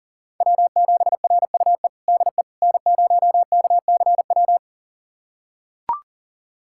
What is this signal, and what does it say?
Morse code, heard as a keyed tone, about 37 wpm, 700 Hz. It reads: W7RUE DE N0KCW
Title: USA Callsign DE Callsign - 5 Characters - 40wpm